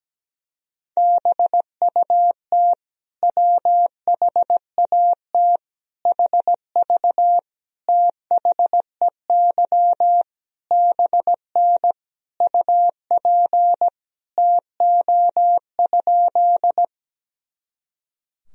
Morse code, heard as a keyed tone, about 17 wpm, 700 Hz. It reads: BUT WHAT HV THEY BN UP TO?
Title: Sentences from Top 100 Words and Abbreviations 17wpm